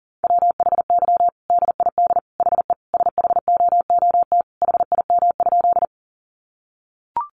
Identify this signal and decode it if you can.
Morse code, heard as a keyed tone, about 40 wpm, 700 Hz. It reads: WHY DID HE SHOOT HIM?
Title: Sentences from Top 1200 Words 40wpm